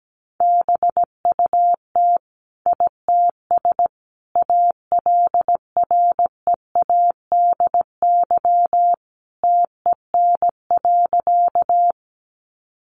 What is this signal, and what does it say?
Morse code, heard as a keyed tone, about 17 wpm, 700 Hz. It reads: BUT ITS ALREADY TEN.
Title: Sentences from Top 700 Words 17wpm